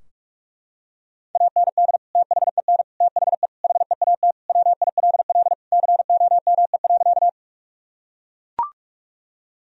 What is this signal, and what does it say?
Morse code, heard as a keyed tone, about 41 wpm, 700 Hz. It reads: AND THEN THE HEAT WILL COME.
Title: Sentences from Top 1000 Words - Review Part 3 - 45wpm